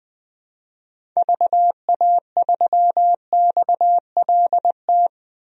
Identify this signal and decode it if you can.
Morse code, heard as a keyed tone, about 21 wpm, 700 Hz. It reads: VA3XLT
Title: DXCC-100 Call Signs - 6 Characters - Repeated 3x - 20wpm